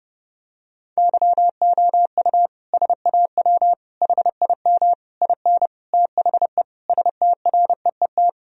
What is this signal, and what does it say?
Morse code, heard as a keyed tone, about 30 wpm, 700 Hz. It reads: YOU SAW HIM IN THE STREET
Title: Sentences from Top 1000 Words - Review Part 3 - 30wpm